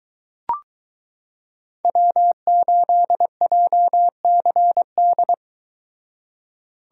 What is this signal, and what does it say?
Morse code, heard as a keyed tone, about 22 wpm, 700 Hz. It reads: W8JCD